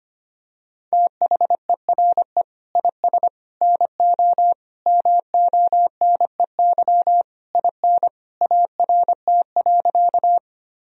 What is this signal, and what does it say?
Morse code, heard as a keyed tone, about 25 wpm, 700 Hz. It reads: THERE IS NO MONEY IN ART.
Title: Sentences from Top 1000 Words - Review Part 3 - 25wpm